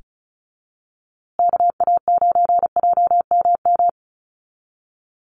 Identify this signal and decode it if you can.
Morse code, heard as a keyed tone, about 34 wpm, 700 Hz. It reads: KA9JMM
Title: USA Call Signs Repeated 2x 35wpm